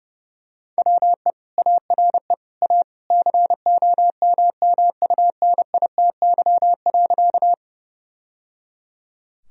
Morse code, heard as a keyed tone, about 28 wpm, 700 Hz. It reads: WE ARE A COMMUNITY.
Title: Sentences from Top 1000 Words - Review Part 4 - 30wpm